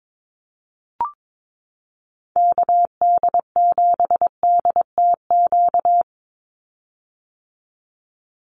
Morse code, heard as a keyed tone, about 22 wpm, 700 Hz. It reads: KD7DTQ
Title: USA Call Signs 22wpm